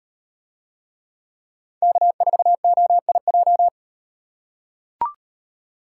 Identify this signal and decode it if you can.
Morse code, heard as a keyed tone, about 38 wpm, 700 Hz. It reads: K4OIJ